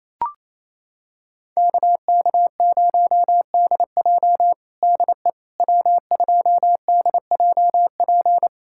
Morse code, heard as a keyed tone, about 28 wpm, 700 Hz. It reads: KK0DJ DE W2DJP
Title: USA Callsign DE Callsign - 5 Characters - 28wpm